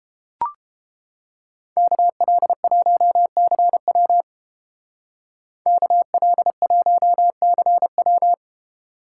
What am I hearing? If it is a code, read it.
Morse code, heard as a keyed tone, about 29 wpm, 700 Hz. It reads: KL1CW KL1CW